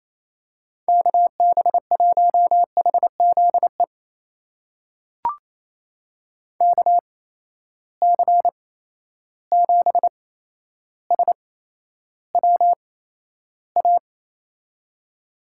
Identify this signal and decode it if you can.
Morse code, heard as a keyed tone, about 28 wpm, spaced as for 12 wpm, 700 Hz. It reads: KB1HZE KC7SWA
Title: USA Call Signs - 6 Characters - 8x Character Spacing - 28wpm